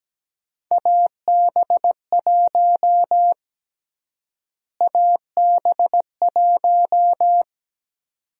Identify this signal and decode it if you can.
Morse code, heard as a keyed tone, about 17 wpm, 700 Hz. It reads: AB1 AB1
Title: USA Call Sign Prefixes - Repeated 2x - 17wpm